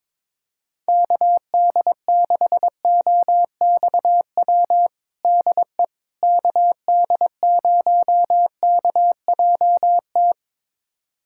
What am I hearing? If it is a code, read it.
Morse code, heard as a keyed tone, about 21 wpm, 700 Hz. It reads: KD6OXW DE KD0KJT
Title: USA Callsign DE Callsign - 6 Characters - 22wpm